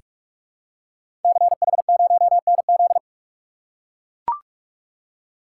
Morse code, heard as a keyed tone, about 44 wpm, 700 Hz. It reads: CH0NZ
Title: Code Groups - 5 Letter:Numbers - 45wpm